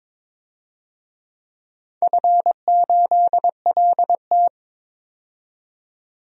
Morse code, heard as a keyed tone, about 22 wpm, 700 Hz. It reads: F8LT
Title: DXCC-100 Call Signs - 4 Characters - 22wpm